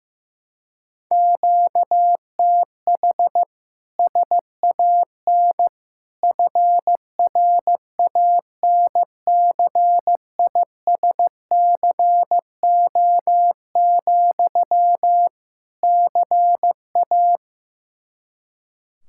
Morse code, heard as a keyed tone, about 15 wpm, 700 Hz. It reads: QTH SAN FRANCISCO, CA
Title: Practice QSO - Exchange 1 - Beginning - 15wpm